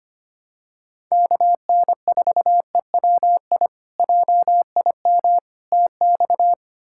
Morse code, heard as a keyed tone, about 25 wpm, 700 Hz. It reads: KN4EWI JIM TX
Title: K1USN SST QSO Practice - 25wpm